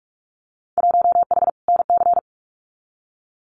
Morse code, heard as a keyed tone, about 46 wpm, 700 Hz. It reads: 1H NC
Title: Winter Field Day Exchanges 45wpm